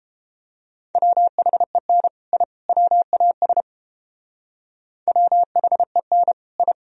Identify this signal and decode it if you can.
Morse code, heard as a keyed tone, about 29 wpm, 700 Hz. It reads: WHEN I WAS WHEN I